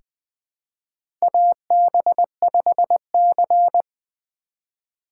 Morse code, heard as a keyed tone, about 20 wpm, 700 Hz. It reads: AB5C
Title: K1USN SST QSO Practice - 20wpm